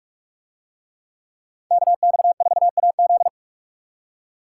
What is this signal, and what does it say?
Morse code, heard as a keyed tone, about 42 wpm, 700 Hz. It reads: KX4AZ